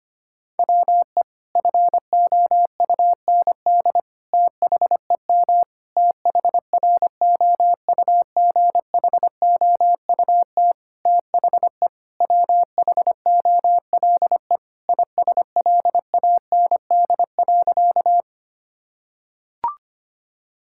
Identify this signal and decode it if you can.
Morse code, heard as a keyed tone, about 25 wpm, 700 Hz. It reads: WE FOUND THEM THROUGHOUT THE WHOLE ISLAND.
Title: Sentences from Top 1000 Words - Review Part 5 - 25wpm